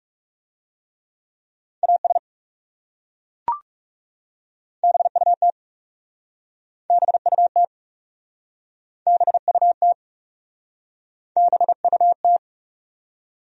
Morse code, heard as a keyed tone, about 40 wpm, 700 Hz. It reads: AS BUT BUT BUT BUT